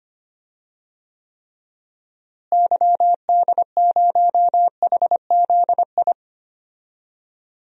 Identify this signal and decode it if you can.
Morse code, heard as a keyed tone, about 25 wpm, 700 Hz. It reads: YD0HZI
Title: DXCC-100 Call Signs - 6 Characters - Repeated 3x - 25wpm